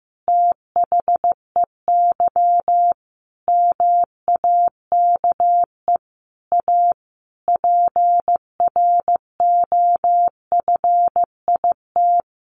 Morse code, heard as a keyed tone, about 15 wpm, 700 Hz. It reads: THEY MAKE A PROFIT
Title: Sentences from Top 800 Words 15wpm